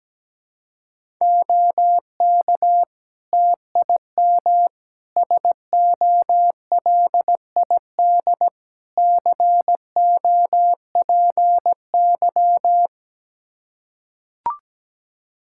Morse code, heard as a keyed tone, about 17 wpm, 700 Hz. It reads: OK TIM SOLID COPY
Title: Practice QSO - Exchange 3 - More Info - 17wpm